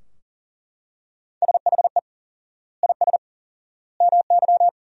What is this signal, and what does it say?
Morse code, heard as a keyed tone, about 40 wpm, 700 Hz. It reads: SHE IS MY
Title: 4x Word-Spacing - Sentences from Top 300 Words - 40wpm